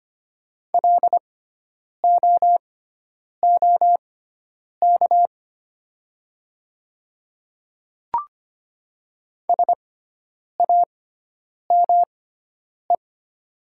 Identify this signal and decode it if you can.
Morse code, heard as a keyed tone, about 25 wpm, spaced as for 9 wpm, 700 Hz. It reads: LOOK SAME
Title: 6x Character Spacing - Top 300 Words - 25wpm